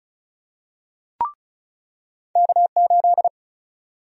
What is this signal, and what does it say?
Morse code, heard as a keyed tone, about 36 wpm, 700 Hz. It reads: K8
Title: USA Call Sign Prefixes - 35wpm